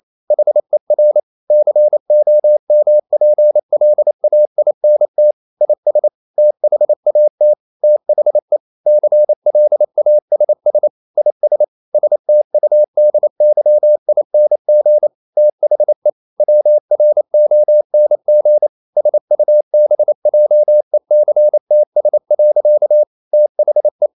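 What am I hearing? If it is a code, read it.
Morse code, heard as a keyed tone, about 28 wpm, 600 Hz. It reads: HER COMPLAINT IS THAT THE CLASS IS STUDYING THE WRONG SUBJECTS. THE
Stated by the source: Morse code practice at 28 wpm, text of Ken Follett's novel Never in English, from CW med Gnister.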